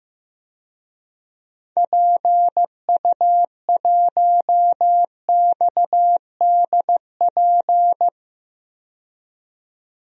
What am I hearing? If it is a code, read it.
Morse code, heard as a keyed tone, about 15 wpm, 700 Hz. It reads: PU1XDP
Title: DXCC-100 Call Signs - 6 Characters - 15wpm